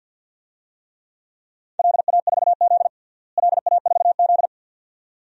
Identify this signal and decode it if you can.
Morse code, heard as a keyed tone, about 47 wpm, 700 Hz. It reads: LA4Z LA4Z
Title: DXCC-100 Call Signs - 4 Characters - Repeated 2x - 50wpm